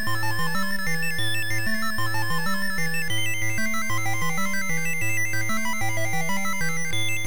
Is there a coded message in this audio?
No.